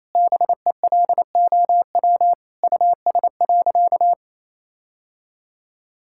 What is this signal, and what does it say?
Morse code, heard as a keyed tone, about 28 wpm, 700 Hz. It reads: BELOW US.
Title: Sentences from Top 1000 Words - Review Part 5 - 28wpm